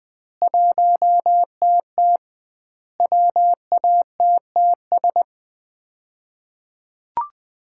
Morse code, heard as a keyed tone, about 20 wpm, 700 Hz. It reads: 1TT WATTS
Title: Common QSO Phrases - 2x Word-Spacing - 20wpm